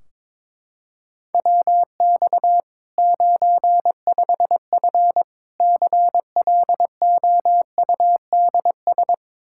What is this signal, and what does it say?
Morse code, heard as a keyed tone, about 22 wpm, 700 Hz. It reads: WX 95F CLOUDS
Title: K1USN SST QSO Practice 22wpm